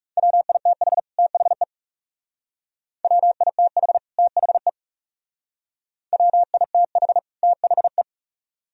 Morse code, heard as a keyed tone, about 38 wpm, 700 Hz. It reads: WITH THE WITH THE WITH THE